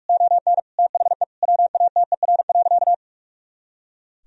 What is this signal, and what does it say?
Morse code, heard as a keyed tone, about 44 wpm, 700 Hz. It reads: ON THE WATER.
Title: Sentences from Top 500 Words and Abbreviations 45wpm